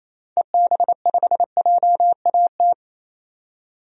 Morse code, heard as a keyed tone, about 28 wpm, 700 Hz. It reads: EB5JAT
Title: DXCC-100 Call Signs - 6 Characters - 28wpm